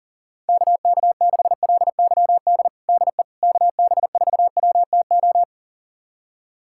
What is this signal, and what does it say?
Morse code, heard as a keyed tone, about 40 wpm, 690 Hz. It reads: KK6LYD DE KB4WTO
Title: USA Callsign DE Callsign - 6 Characters - 40wpm